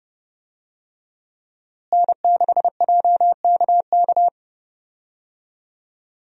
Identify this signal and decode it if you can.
Morse code, heard as a keyed tone, about 32 wpm, 700 Hz. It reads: N6JKK
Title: USA Call Signs - 5 Characters - Repeated 3x - 30wpm